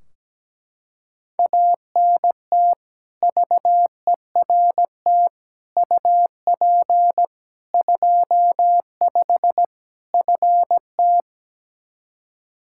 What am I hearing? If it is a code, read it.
Morse code, heard as a keyed tone, about 17 wpm, 700 Hz. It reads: ANT VERT UP 25 FT
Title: K1USN SST QSO Practice 17wpm